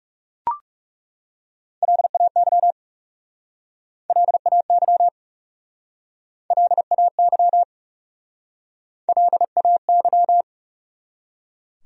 Morse code, heard as a keyed tone, about 40 wpm, 700 Hz. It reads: LAY LAY LAY LAY